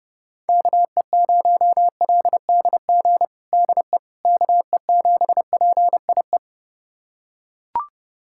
Morse code, heard as a keyed tone, about 30 wpm, 700 Hz. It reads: KE0LDG DE KE7PIE
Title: USA Callsign DE Callsign - 6 Characters - 30wpm